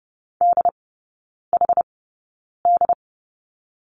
Morse code, heard as a keyed tone, about 30 wpm, spaced as for 9 wpm, 700 Hz. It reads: DHD
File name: course / 30:10wpm / 087 Lesson 15e - Number 5 - Three Characters 30wpm